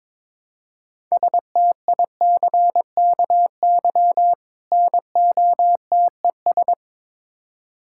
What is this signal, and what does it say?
Morse code, heard as a keyed tone, about 21 wpm, 700 Hz. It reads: STICKY NOTES
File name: Difficult Words with Hints - 22wpm